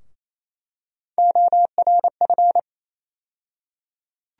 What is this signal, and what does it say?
Morse code, heard as a keyed tone, about 28 wpm, 700 Hz. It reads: ORF
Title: USA Call Sign Postfixes - 3 Letters 28wpm